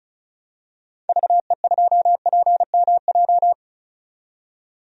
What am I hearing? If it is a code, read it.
Morse code, heard as a keyed tone, about 34 wpm, 700 Hz. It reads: VE2PMJ
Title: DXCC-100 Call Signs - 6 Characters - 35wpm